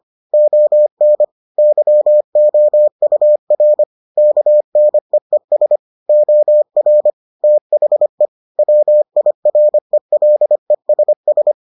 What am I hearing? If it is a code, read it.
Morse code, heard as a keyed tone, about 25 wpm, 600 Hz. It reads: ON YOUR KNEES OR THE WIRELESS